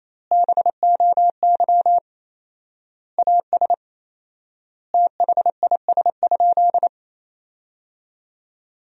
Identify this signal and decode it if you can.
Morse code, heard as a keyed tone, about 28 wpm, 700 Hz. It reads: BOY AS THIS?
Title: 4x Word-Spacing - Sentences from Top 400 Words - 28wpm